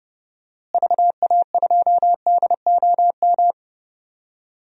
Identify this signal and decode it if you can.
Morse code, heard as a keyed tone, about 32 wpm, 700 Hz. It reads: VA2DOM
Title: DXCC-100 Call Signs - 6 Characters - Repeated 3x - 30wpm